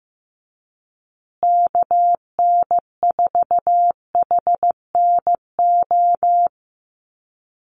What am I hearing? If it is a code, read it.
Morse code, heard as a keyed tone, about 15 wpm, 700 Hz. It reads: KN4HNO
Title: USA Call Signs Repeated 3x 15wpm